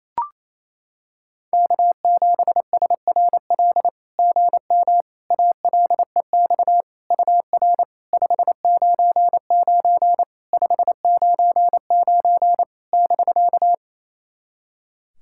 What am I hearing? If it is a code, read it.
Morse code, heard as a keyed tone, about 28 wpm, 700 Hz. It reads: K7SRL GM ALEX UR 599 599 <BK>
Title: SOTA QSO Practice 28wpm